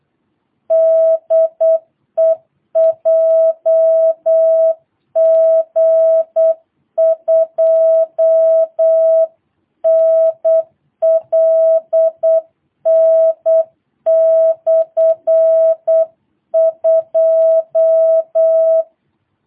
Morse code, heard as a keyed tone, about 8 wpm, 650 Hz. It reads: DE JG2NLN/2
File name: jh8cbh / 2022cw